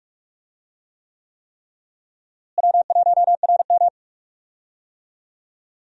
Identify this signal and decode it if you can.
Morse code, heard as a keyed tone, about 46 wpm, 700 Hz. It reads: W1RM
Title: USA Call Signs - 4 Characters - Repeated 3x - 45wpm